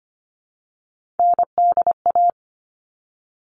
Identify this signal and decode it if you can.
Morse code, heard as a keyed tone, about 25 wpm, 700 Hz. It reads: NDA
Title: All Cut Numbers - 3-Digit 25wpm